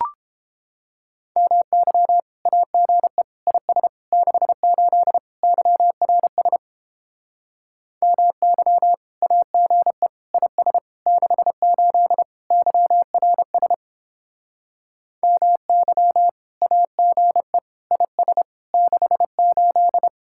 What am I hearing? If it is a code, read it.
Morse code, heard as a keyed tone, about 31 wpm, 700 Hz. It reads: MY AGE IS 68 YRS MY AGE IS 68 YRS MY AGE IS 68